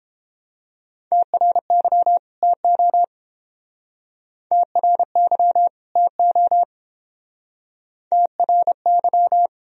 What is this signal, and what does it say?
Morse code, heard as a keyed tone, about 28 wpm, 700 Hz. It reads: TRY TO TRY TO TRY